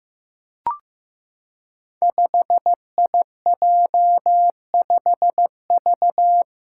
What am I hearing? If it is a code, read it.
Morse code, heard as a keyed tone, about 15 wpm, 700 Hz. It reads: 5IJ5V
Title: Five Challenging Letter-Number Combinations 15wpm